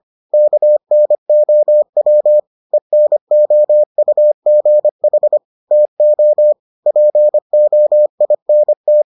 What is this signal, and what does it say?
Morse code, heard as a keyed tone, about 25 wpm, 600 Hz. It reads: KNOW ENOUGH TO POINT